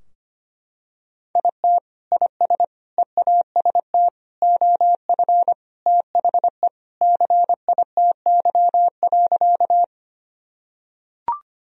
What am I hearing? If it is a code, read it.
Morse code, heard as a keyed tone, about 25 wpm, 700 Hz. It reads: IT IS EAST OF THE CITY.
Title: Sentences from Top 1000 Words - Review Part 3 - 25wpm